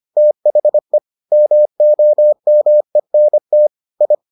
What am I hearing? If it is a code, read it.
Morse code, heard as a keyed tone, about 25 wpm, 600 Hz. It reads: THE MOMENT I